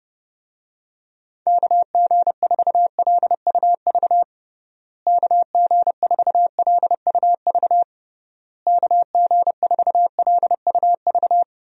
Morse code, heard as a keyed tone, about 30 wpm, 700 Hz. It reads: KG4LUV KG4LUV KG4LUV
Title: USA Call Signs - 6 Characters - Repeated 3x - 30wpm